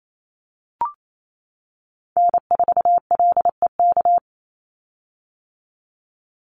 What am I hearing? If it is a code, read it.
Morse code, heard as a keyed tone, about 28 wpm, 700 Hz. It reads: N4LEK
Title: USA Call Signs 28wpm